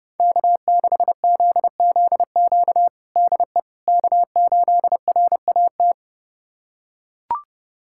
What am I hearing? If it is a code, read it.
Morse code, heard as a keyed tone, about 32 wpm, 700 Hz. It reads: K6ZZQ DE K8RAT
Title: USA Callsign DE Callsign 30wpm